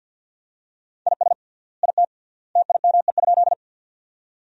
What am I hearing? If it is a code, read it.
Morse code, heard as a keyed tone, about 53 wpm, 700 Hz. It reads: IS IT TIME?
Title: 3x Word-Spacing - Sentences from Top 100 Words - 50wpm